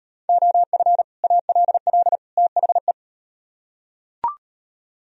Morse code, heard as a keyed tone, about 38 wpm, 700 Hz. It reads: OF ALL THE